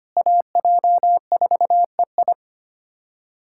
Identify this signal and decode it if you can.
Morse code, heard as a keyed tone, about 25 wpm, 700 Hz. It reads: AJ4EI
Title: USA Callsign DE Callsign - 5 Characters - 25wpm